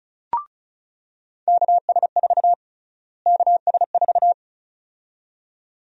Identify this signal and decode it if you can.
Morse code, heard as a keyed tone, about 36 wpm, 700 Hz. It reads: KS4 KS4
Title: USA Call Sign Prefixes - Repeated 2x - 35wpm